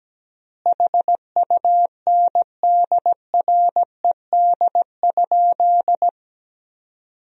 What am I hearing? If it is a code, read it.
Morse code, heard as a keyed tone, about 17 wpm, 700 Hz. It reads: HUNDRED?
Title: 3x Word-Spacing - Sentences from Top 400 Words - 17wpm